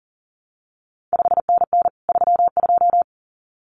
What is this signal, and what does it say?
Morse code, heard as a keyed tone, about 42 wpm, 700 Hz. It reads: 5NN 32
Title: CQ WW DX Contest Exchange Encore - 40wpm